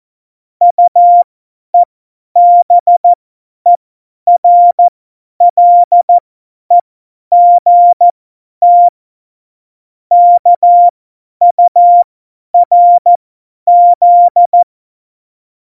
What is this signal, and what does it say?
Morse code, heard as a keyed tone, about 14 wpm, spaced as for 10 wpm, 700 Hz. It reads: UEBERLEGT KURZ
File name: Morsekurs / Witze / Witz 10. Oktober 17, Tempo 50